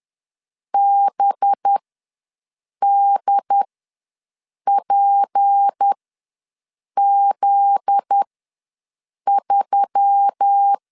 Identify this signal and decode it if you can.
Morse code, heard as a keyed tone, about 10 wpm, spaced as for 6 wpm, 800 Hz. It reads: BDPZ3
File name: LEZIONI / 9 2867 ws 10wpm 4215